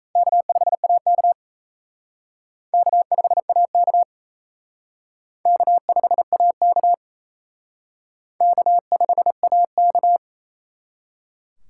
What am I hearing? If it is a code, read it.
Morse code, heard as a keyed tone, about 37 wpm, 700 Hz. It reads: K5AK K5AK K5AK K5AK